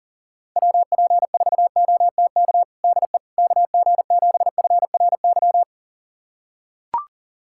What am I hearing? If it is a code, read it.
Morse code, heard as a keyed tone, about 40 wpm, 700 Hz. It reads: WP4OTK DE KG7FRY
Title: USA Callsign DE Callsign - 6 Characters - 40wpm